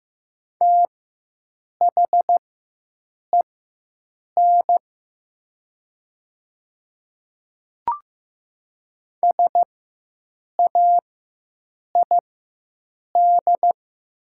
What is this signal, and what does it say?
Morse code, heard as a keyed tone, about 15 wpm, spaced as for 7 wpm, 700 Hz. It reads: THEN SAID
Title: 4x Character Spacing - Top 300 Words - 15wpm